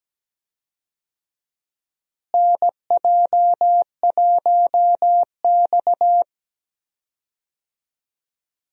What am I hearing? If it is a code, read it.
Morse code, heard as a keyed tone, about 17 wpm, 700 Hz. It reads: NJ1X
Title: USA Call Signs - 4 Characters - Repeated 3x - 17wpm